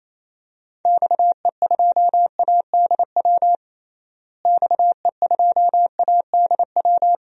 Morse code, heard as a keyed tone, about 28 wpm, 700 Hz. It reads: XE2ADW XE2ADW
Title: DXCC-100 Call Signs - 6 Characters - Repeated 3x - 28wpm